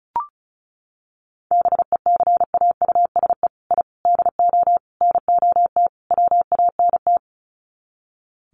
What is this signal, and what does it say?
Morse code, heard as a keyed tone, about 34 wpm, 700 Hz. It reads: BECAUSE I DO NOT WANT
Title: Sets of 5 Words Encore 35wpm